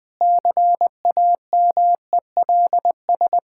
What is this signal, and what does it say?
Morse code, heard as a keyed tone, about 21 wpm, 700 Hz. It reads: CAMELS
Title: Sentences with 7-Word Phrases - Spoken Beginning - 20wpm